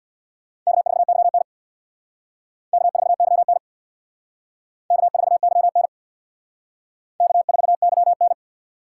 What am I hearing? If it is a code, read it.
Morse code, heard as a keyed tone, about 64 wpm, 700 Hz. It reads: K4YN K4YN K4YN K4YN